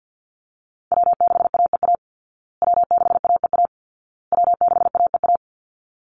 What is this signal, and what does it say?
Morse code, heard as a keyed tone, about 50 wpm, 700 Hz. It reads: W6AEA W6AEA W6AEA
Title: Call Signs Repeated 2x - 50wpm